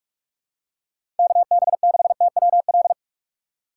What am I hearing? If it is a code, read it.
Morse code, heard as a keyed tone, about 46 wpm, 700 Hz. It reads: KB6TWL
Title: USA Call Signs - 2x Character Spacing - 45wpm